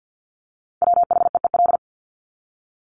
Morse code, heard as a keyed tone, about 50 wpm, 700 Hz. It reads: WHEEL
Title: Top 500 Words 50wpm